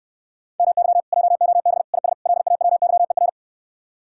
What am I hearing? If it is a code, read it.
Morse code, heard as a keyed tone, about 69 wpm, 700 Hz. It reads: MY JOB IS LAWYER